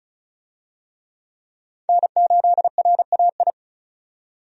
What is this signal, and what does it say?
Morse code, heard as a keyed tone, about 36 wpm, 700 Hz. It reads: N8RAI
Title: USA Call Signs - 5 Characters - 35wpm